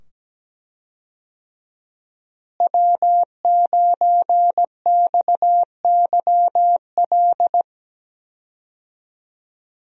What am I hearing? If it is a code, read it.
Morse code, heard as a keyed tone, about 17 wpm, 700 Hz. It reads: W9XYL
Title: USA Call Signs - 5 Characters - Repeated 2x - 17wpm